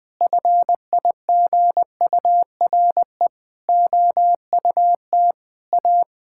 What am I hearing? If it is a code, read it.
Morse code, heard as a keyed tone, about 20 wpm, 700 Hz. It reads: FIGURE OUT A
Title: Sentences with 6-Word Phrases - 20wpm